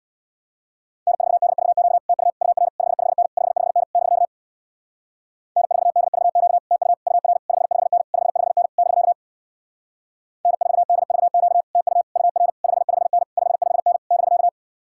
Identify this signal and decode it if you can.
Morse code, heard as a keyed tone, about 66 wpm, 700 Hz. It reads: N4DVQ TU UR 55N 55N <BK> N4DVQ TU UR 55N 55N <BK> N4DVQ TU UR 55N 55N <BK>